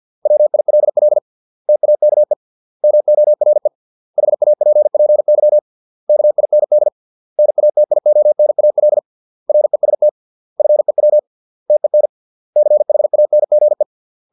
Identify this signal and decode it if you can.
Morse code, heard as a keyed tone, about 50 wpm, 600 Hz. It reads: WILL TAKE MORE HAPPY KIND NATIONAL REST FEW TEN CHANGE